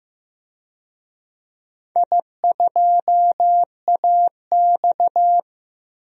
Morse code, heard as a keyed tone, about 15 wpm, 700 Hz. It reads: I2AX
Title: DXCC-100 Call Signs - 4 Characters - 15wpm